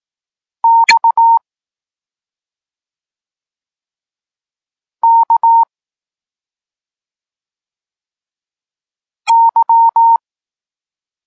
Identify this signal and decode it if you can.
Morse code, heard as a keyed tone, about 18 wpm, 920 Hz. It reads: X K Y
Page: Tekst 32 – Doskonalenie odbioru poznanych liter